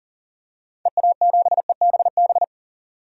Morse code, heard as a keyed tone, about 38 wpm, 700 Hz. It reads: EA7EBB
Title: DXCC-100 Call Signs - 6 Characters - 40wpm